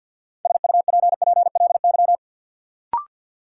Morse code, heard as a keyed tone, about 52 wpm, 700 Hz. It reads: SUPPLY
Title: Sentences from Top 1200 Words - New Word Drill Alt - 50wpm